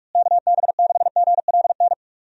Morse code, heard as a keyed tone, about 41 wpm, 700 Hz. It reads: KB6GLN
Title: USA Call Signs - 6 Characters - Repeated 3x - 45wpm